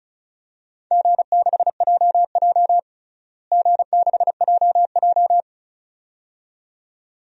Morse code, heard as a keyed tone, about 36 wpm, 700 Hz. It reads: G6JJ G6JJ
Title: DXCC-100 Call Signs - 4 Characters - Repeated 2x - 35wpm